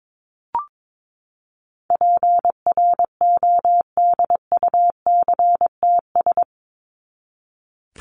Morse code, heard as a keyed tone, about 22 wpm, 700 Hz. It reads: PRODUCTS
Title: ES:S Suffix - Top 100 Words with Base Words 22wpm